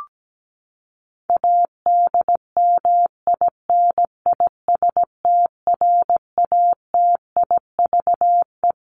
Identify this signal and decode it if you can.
Morse code, heard as a keyed tone, about 17 wpm, 700 Hz. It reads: ADMINISTRATIVE
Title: Top 1501-1750 Words 17wpm